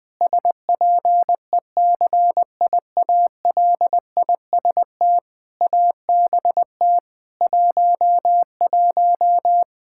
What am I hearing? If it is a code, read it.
Morse code, heard as a keyed tone, about 20 wpm, 700 Hz. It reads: SPECIALIST ABT 11
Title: Practice QSO - Exchange 3 - More Info - 20wpm